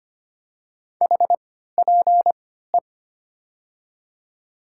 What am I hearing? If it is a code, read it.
Morse code, heard as a keyed tone, about 25 wpm, 700 Hz. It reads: H P E
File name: Top 90 QSO Elements - 3x Character Spacing - 25wpm